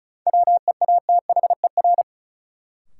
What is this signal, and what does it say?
Morse code, heard as a keyed tone, about 34 wpm, 700 Hz. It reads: WEATHER
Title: Sentences from Top 800 Words - New Word Drill - 35wpm